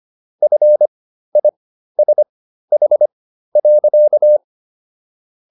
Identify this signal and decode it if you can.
Morse code, heard as a keyed tone, about 25 wpm, spaced as for 13 wpm, 600 Hz. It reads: FISH.